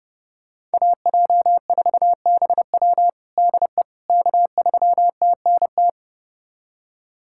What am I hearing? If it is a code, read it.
Morse code, heard as a keyed tone, about 30 wpm, 700 Hz. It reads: AJ4BW DE K3TNT
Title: USA Callsign DE Callsign - 5 Characters - 30wpm